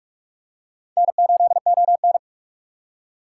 Morse code, heard as a keyed tone, about 46 wpm, 700 Hz. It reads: N8ON
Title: USA Call Signs - 4 Characters - 2x Character Spacing - 45wpm